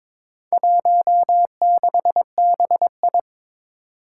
Morse code, heard as a keyed tone, about 22 wpm, 700 Hz. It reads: 16BI
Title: Four Challenging Letter-Number Combinations 22wpm